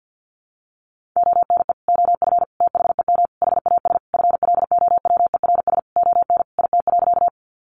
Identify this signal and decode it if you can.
Morse code, heard as a keyed tone, about 50 wpm, 700 Hz. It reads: ONE OF THEM HAS FLOWERS ON IT.
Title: Additional Easy Phrases 50wpm